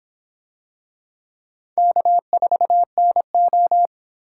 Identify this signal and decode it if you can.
Morse code, heard as a keyed tone, about 26 wpm, 700 Hz. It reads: K4NO